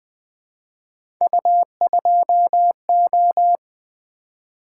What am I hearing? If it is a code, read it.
Morse code, heard as a keyed tone, about 20 wpm, 700 Hz. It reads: U2O
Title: Three Letter-Number Combinations - 20wpm